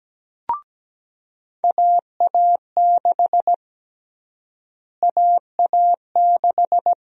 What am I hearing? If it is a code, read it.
Morse code, heard as a keyed tone, about 17 wpm, 700 Hz. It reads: AA6 AA6
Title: USA Call Sign Prefixes - Repeated 2x - 17wpm